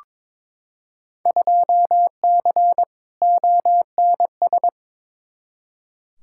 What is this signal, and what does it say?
Morse code, heard as a keyed tone, about 22 wpm, 700 Hz. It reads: 2C ONS
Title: Field Day Exchanges Encore - 22wpm